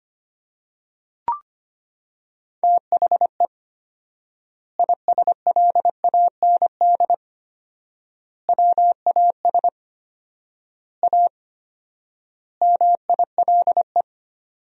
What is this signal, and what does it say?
Morse code, heard as a keyed tone, about 25 wpm, 700 Hz. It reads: THE ISLAND WAS A MILE
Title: 4x Word-Spacing - Sentences from Top 500 Words - 25wpm